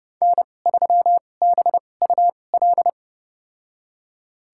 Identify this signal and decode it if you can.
Morse code, heard as a keyed tone, about 30 wpm, spaced as for 22 wpm, 700 Hz. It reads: N3BUL
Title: USA Call Signs - 2x Character Spacing - 30wpm